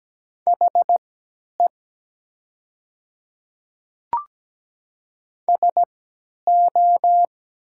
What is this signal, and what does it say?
Morse code, heard as a keyed tone, about 17 wpm, spaced as for 10 wpm, 700 Hz. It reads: HE SO
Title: 3x Character Spacing - Top 100 Words - 17wpm